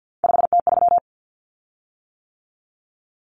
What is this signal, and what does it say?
Morse code, heard as a keyed tone, about 50 wpm, 700 Hz. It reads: 5T3
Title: Common Cut Numbers - 3-Digit 50wpm